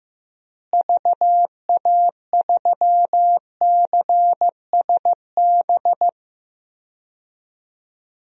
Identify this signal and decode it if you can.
Morse code, heard as a keyed tone, about 15 wpm, 700 Hz. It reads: VA3CSB
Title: DXCC-100 Call Signs - 6 Characters - Repeated 3x - 15wpm